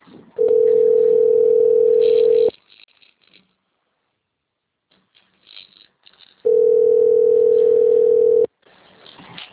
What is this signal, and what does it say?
Ringback tone (North America), 440 + 480 Hz.
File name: rg-1600-unknown- - - .81843